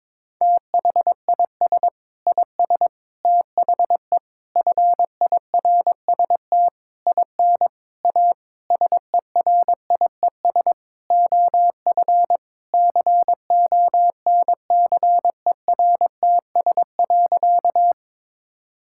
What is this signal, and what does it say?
Morse code, heard as a keyed tone, about 22 wpm, 700 Hz. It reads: THIS IS THE FIRST IN A SERIES OF CONCERTS.
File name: practice / 22wpm / Sentences with 8-Word Phrases - 22wpm